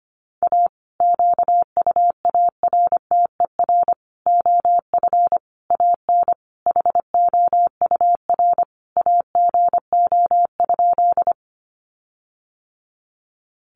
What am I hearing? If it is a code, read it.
Morse code, heard as a keyed tone, about 25 wpm, 700 Hz. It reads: A QUARTER OF AN HOUR AGO?
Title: Sentences from Top 1300 Words 25wpm